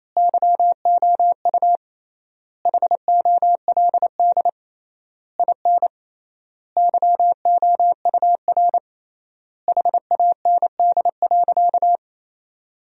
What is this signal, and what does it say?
Morse code, heard as a keyed tone, about 28 wpm, 700 Hz. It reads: YOU HOLD IN YOUR HAND.
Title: 3x Word-Spacing - Sentences from Top 400 Words - 28wpm